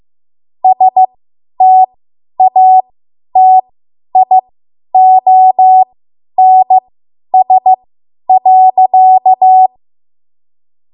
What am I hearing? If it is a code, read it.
Morse code, heard as a keyed tone, about 15 wpm, spaced as for 10 wpm, 750 Hz. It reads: STATIONS.
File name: Morse / 10WPM